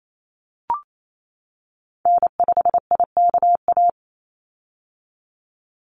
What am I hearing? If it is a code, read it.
Morse code, heard as a keyed tone, about 28 wpm, 700 Hz. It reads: N5IKA
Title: USA Call Signs 28wpm